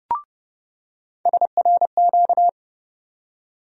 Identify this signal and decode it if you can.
Morse code, heard as a keyed tone, about 31 wpm, 700 Hz. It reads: SRQ
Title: USA Call Sign Postfixes - 3 Letters 30wpm